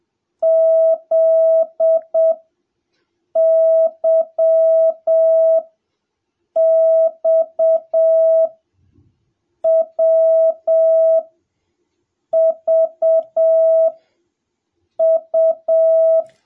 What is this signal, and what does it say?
Morse code, heard as a keyed tone, about 7 wpm, spaced as for 5 wpm, 650 Hz. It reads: ZYXWVU